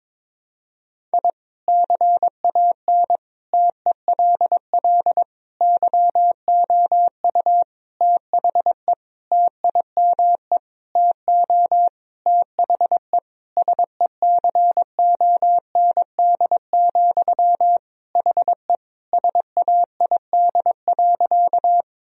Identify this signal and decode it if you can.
Morse code, heard as a keyed tone, about 22 wpm, 700 Hz. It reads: I CAN TELL YOU THE TIME TO THE SECOND, HE SAID.
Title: Sentences from Top 1000 Words - Review Part 1 - 22wpm